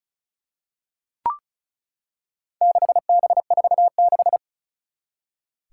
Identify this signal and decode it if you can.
Morse code, heard as a keyed tone, about 36 wpm, 690 Hz. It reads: 6B46
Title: Four Challenging Letter-Number Combinations 35wpm